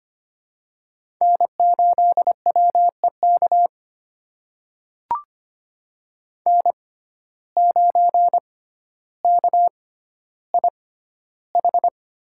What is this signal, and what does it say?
Morse code, heard as a keyed tone, about 25 wpm, spaced as for 13 wpm, 700 Hz. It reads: N8WEK N9KIH
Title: USA Call Signs - 5 Characters - 6x Character Spacing - 25wpm